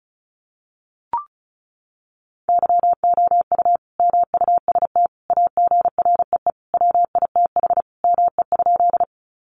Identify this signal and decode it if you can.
Morse code, heard as a keyed tone, about 34 wpm, 700 Hz. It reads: YOU MUST AGREE WITH ME?
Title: Sentences from Top 1100 Words 35wpm